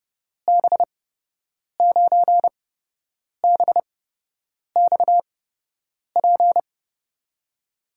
Morse code, heard as a keyed tone, about 30 wpm, spaced as for 8 wpm, 700 Hz. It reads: B9BXP
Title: USA Call Signs - 6 Characters - 8x Character Spacing - 30wpm